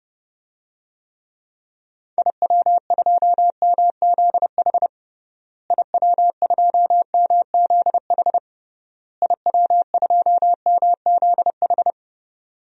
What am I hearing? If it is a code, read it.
Morse code, heard as a keyed tone, about 30 wpm, 700 Hz. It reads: IW2MZH IW2MZH IW2MZH
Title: DXCC-100 Call Signs - 6 Characters - Repeated 3x - 30wpm